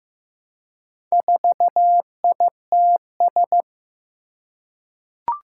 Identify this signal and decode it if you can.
Morse code, heard as a keyed tone, about 15 wpm, 700 Hz. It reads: 4ITS
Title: Four Challenging Letter-Number Combinations 15wpm